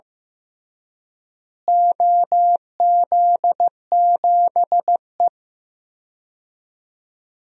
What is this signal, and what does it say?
Morse code, heard as a keyed tone, about 15 wpm, 700 Hz. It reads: OZ7E
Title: DXCC-100 Call Signs - 4 Characters - Repeated 3x - 15wpm